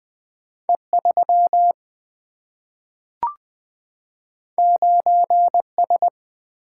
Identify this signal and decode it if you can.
Morse code, heard as a keyed tone, about 20 wpm, 700 Hz. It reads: E3 9S
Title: Two Letter-Number Combinations - 20wpm